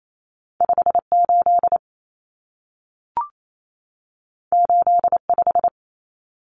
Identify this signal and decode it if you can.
Morse code, heard as a keyed tone, about 28 wpm, 700 Hz. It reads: 58 85
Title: Common Cut Numbers - 2-Digit 28wpm